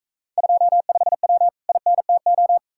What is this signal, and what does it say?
Morse code, heard as a keyed tone, about 43 wpm, 700 Hz. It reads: 25W INTO